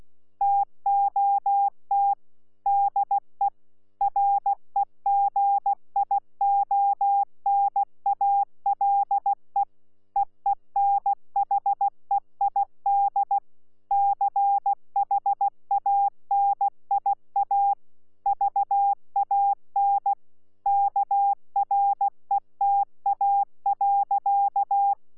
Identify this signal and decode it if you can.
Morse code, heard as a keyed tone, about 16 wpm, 800 Hz. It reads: TOT DE REGIONALE EENHEID CHANIA VAN KRETA.
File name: Z PI4VRZ / PI4VRZ 14-jun-2025